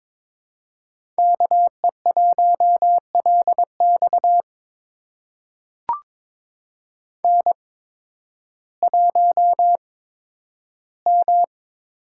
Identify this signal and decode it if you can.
Morse code, heard as a keyed tone, about 22 wpm, spaced as for 12 wpm, 700 Hz. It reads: KE1LX N1M
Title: USA Call Signs - 5 Characters - 8x Character Spacing - 22wpm